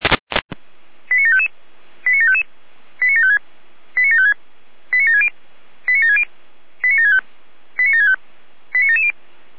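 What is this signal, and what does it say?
Five-tone selective call (ZVEI), heard as quick runs of five tones, tones of about 70 ms. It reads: 89755 89755 89756 89756 89760 89760 89765 89765 89770
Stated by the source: ZVEI five-tone selective call received by SDR.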